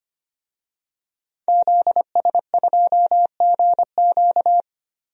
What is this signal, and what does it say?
Morse code, heard as a keyed tone, about 25 wpm, 700 Hz. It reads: ZS2GQ
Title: DXCC-100 Call Signs - 5 Characters - 25wpm